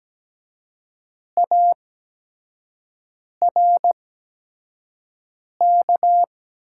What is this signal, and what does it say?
Morse code, heard as a keyed tone, about 17 wpm, spaced as for 5 wpm, 700 Hz. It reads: ARK
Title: 8x Character Spacing - Top 300 Words - 17wpm